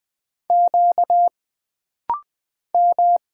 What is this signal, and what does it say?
Morse code, heard as a keyed tone, about 20 wpm, 700 Hz. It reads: Q M
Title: Single Letters - Rapid-Fire 20wpm